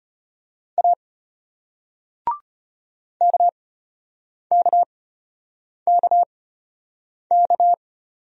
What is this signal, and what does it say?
Morse code, heard as a keyed tone, about 34 wpm, spaced as for 8 wpm, 700 Hz. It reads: A KKKK